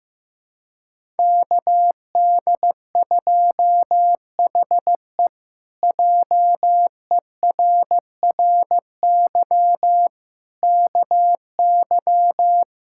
Morse code, heard as a keyed tone, about 15 wpm, 700 Hz. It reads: KD2HE JERRY KY
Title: K1USN SST QSO Practice - 15wpm